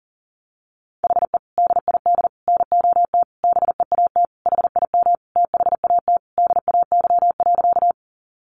Morse code, heard as a keyed tone, about 40 wpm, 700 Hz. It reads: HE DID NOT BEAT HIM THAT DAY.
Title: Sentences from Top 1300 Words 40wpm